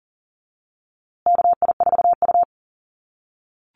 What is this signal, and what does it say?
Morse code, heard as a keyed tone, about 40 wpm, 700 Hz. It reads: KI4U
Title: USA Call Signs Repeated 3x 40wpm